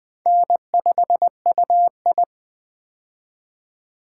Morse code, heard as a keyed tone, about 20 wpm, 700 Hz. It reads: N5UI
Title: K1USN SST QSO Practice - 20wpm